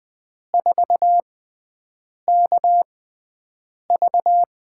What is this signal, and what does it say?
Morse code, heard as a keyed tone, about 20 wpm, spaced as for 7 wpm, 700 Hz. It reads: 4KV
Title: USA Call Signs - 4 Characters - 6x Character Spacing - 20wpm